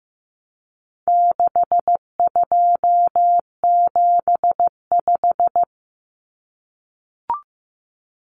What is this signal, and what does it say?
Morse code, heard as a keyed tone, about 15 wpm, 700 Hz. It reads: 6275
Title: Numbers - 4-Digit 15wpm